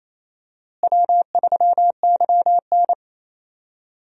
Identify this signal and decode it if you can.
Morse code, heard as a keyed tone, about 28 wpm, 700 Hz. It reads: W3YN
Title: USA Call Signs - 4 Characters - Repeated 3x - 28wpm